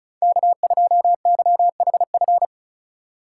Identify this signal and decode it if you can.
Morse code, heard as a keyed tone, about 36 wpm, 700 Hz. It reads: K2YHF
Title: USA Call Signs - 5 Characters - 4x Character Spacing - 35wpm